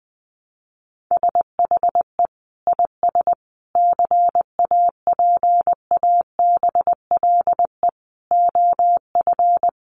Morse code, heard as a keyed tone, about 20 wpm, 700 Hz. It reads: SHE IS CAPABLE OF
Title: Sentences from Top 1200 Words 20wpm